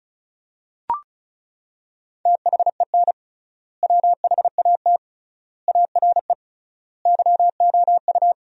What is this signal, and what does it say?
Morse code, heard as a keyed tone, about 36 wpm, 700 Hz. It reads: THEN WHAT ARE YOU
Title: 3x Word-Spacing - Sentences from Top 100 Words - 35wpm